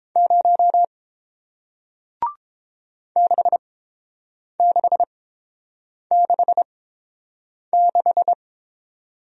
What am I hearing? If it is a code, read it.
Morse code, heard as a keyed tone, about 30 wpm, spaced as for 7 wpm, 700 Hz. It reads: 0 6666